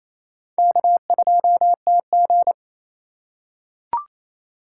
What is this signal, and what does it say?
Morse code, heard as a keyed tone, about 28 wpm, 700 Hz. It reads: K2TG
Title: USA Call Signs - 4 Characters - 3x Character Spacing - 28wpm